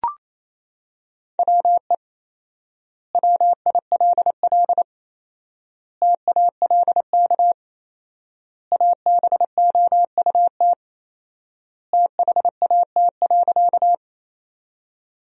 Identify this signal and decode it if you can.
Morse code, heard as a keyed tone, about 28 wpm, 700 Hz. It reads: WE WILL TALK ABOUT THAT.
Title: 4x Word-Spacing - Sentences from Top 400 Words - 28wpm